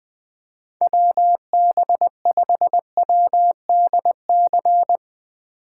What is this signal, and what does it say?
Morse code, heard as a keyed tone, about 20 wpm, 700 Hz. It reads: WB5WDC
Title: USA Call Signs - 6 Characters - 3x Character Spacing - 20wpm